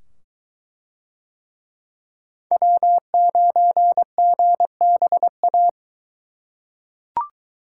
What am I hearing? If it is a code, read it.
Morse code, heard as a keyed tone, about 22 wpm, 700 Hz. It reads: W9GBA